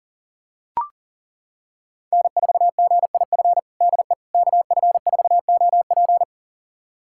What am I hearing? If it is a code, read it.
Morse code, heard as a keyed tone, about 40 wpm, 700 Hz. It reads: N4GIF DE KF4OP
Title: USA Callsign DE Callsign - 5 Characters - 40wpm